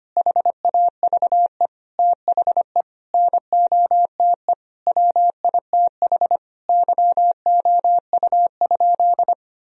Morse code, heard as a keyed tone, about 25 wpm, 700 Hz. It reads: HAVE THE NOTE WITH YOU?
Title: Sentences from Top 1000 Words - Review Part 2 - 25wpm